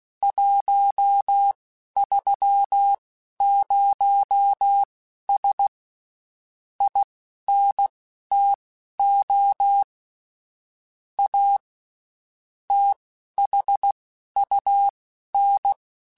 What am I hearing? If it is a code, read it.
Morse code, heard as a keyed tone, about 16 wpm, spaced as for 11 wpm, 800 Hz. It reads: 130S INTO A THUN
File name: QSOs / QSOs-mp3s / WD0BGP 1@15